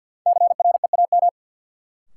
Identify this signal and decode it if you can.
Morse code, heard as a keyed tone, about 47 wpm, 700 Hz. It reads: CREAM
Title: Sentences from Top 1500 Words - New Word Drill Alt - 50wpm